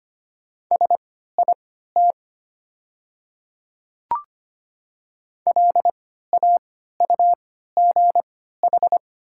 Morse code, heard as a keyed tone, about 25 wpm, spaced as for 14 wpm, 690 Hz. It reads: SIT LAUGH
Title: 3x Character Spacing - Top 500 Words - 25wpm